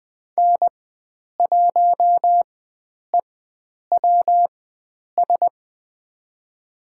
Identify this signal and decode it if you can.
Morse code, heard as a keyed tone, about 20 wpm, spaced as for 9 wpm, 700 Hz. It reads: N1EWS
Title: USA Call Signs - 5 Characters - 4x Character Spacing - 20wpm